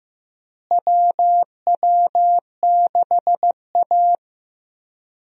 Morse code, heard as a keyed tone, about 15 wpm, 700 Hz. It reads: WW6A
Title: USA Call Signs - 4 Characters - Repeated 2x - 15wpm